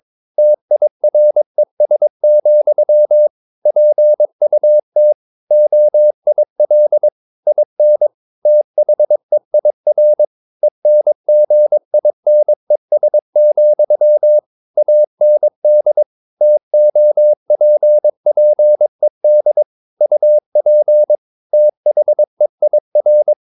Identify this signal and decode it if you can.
Morse code, heard as a keyed tone, about 22 wpm, 600 Hz. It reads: TIRES, PUT OIL IN THEIR ENGINES, AND TOPPED UP THEIR
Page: Romanen Never af Ken Follet på engelsk.